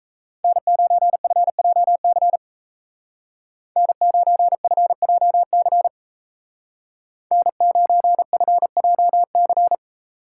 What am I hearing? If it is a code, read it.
Morse code, heard as a keyed tone, about 37 wpm, 700 Hz. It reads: N9FJC N9FJC N9FJC